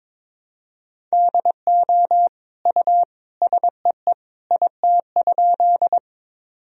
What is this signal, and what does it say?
Morse code, heard as a keyed tone, about 22 wpm, 700 Hz. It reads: DO U SEE IT?
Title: Sentences from Top 100 Words and Abbreviations 22wpm